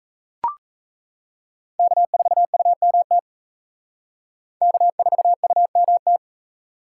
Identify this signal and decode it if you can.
Morse code, heard as a keyed tone, about 34 wpm, 700 Hz. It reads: K4UMT K4UMT
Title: Speed Racing - USA 1x3 Call Signs 28wpm